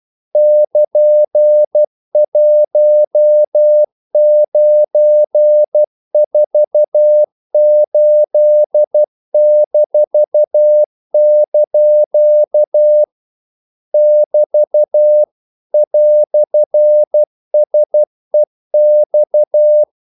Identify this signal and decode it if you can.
Morse code, heard as a keyed tone, about 12 wpm, 600 Hz. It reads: <KN>1948-) <BT> "SEX
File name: CW / Practice / EnglishQuotations0112